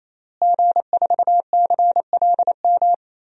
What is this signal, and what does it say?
Morse code, heard as a keyed tone, about 28 wpm, 700 Hz. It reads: G4CLM
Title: DXCC-100 Call Signs - 5 Characters - Repeated 3x - 28wpm